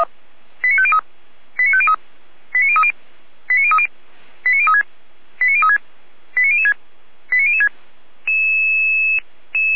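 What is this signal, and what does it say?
Five-tone selective call (ZVEI), heard as quick runs of five tones, tones of about 70 ms. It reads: 89503 89503 89030 89030 89036 89036 89006 89006
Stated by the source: ZVEI five-tone selective call received on an SDR, fire brigade Feuerwehr Orlamünde.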